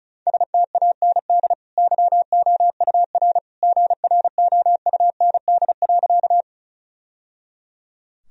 Morse code, heard as a keyed tone, about 36 wpm, 700 Hz. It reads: STAND YOUR GROUND.
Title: Sentences from Top 1000 Words - Review Part 2 - 35wpm